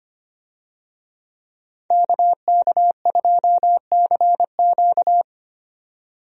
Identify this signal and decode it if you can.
Morse code, heard as a keyed tone, about 25 wpm, 700 Hz. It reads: KK2CQ
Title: USA Call Signs - 5 Characters - 25wpm